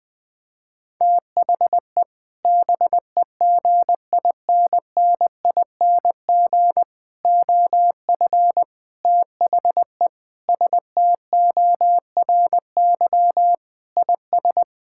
Morse code, heard as a keyed tone, about 20 wpm, 700 Hz. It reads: THE BEGINNING OF THE STORY IS
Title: Sentences from Top 1000 Words - Review Part 6 - 20wpm